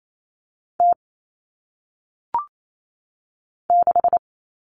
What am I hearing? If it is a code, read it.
Morse code, heard as a keyed tone, about 28 wpm, 700 Hz. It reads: T 6
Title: Common Cut Numbers - 1-Digit 28wpm